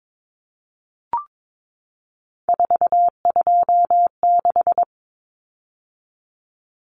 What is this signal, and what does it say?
Morse code, heard as a keyed tone, about 22 wpm, 700 Hz. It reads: 426
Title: Common Cut Numbers - 3-Digit 22wpm